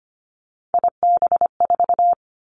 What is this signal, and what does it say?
Morse code, heard as a keyed tone, about 25 wpm, 700 Hz. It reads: IB4
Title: Three Challenging Letter-Number Combinations 25wpm